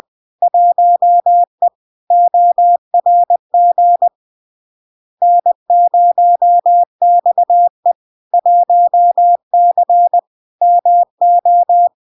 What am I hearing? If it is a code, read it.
Morse code, heard as a keyed tone, about 20 wpm, 700 Hz. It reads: 1E ORG N0XE 1C MO
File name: LARK-FD-2022-Log-in-CW-20wpm-700-Hz